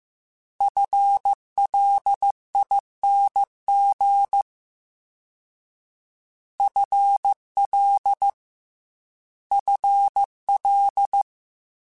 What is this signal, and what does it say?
Morse code, heard as a keyed tone, about 15 wpm, 800 Hz. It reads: FLING FL FL
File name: FL / FL mp3s / FL Prefixes 3x2 @13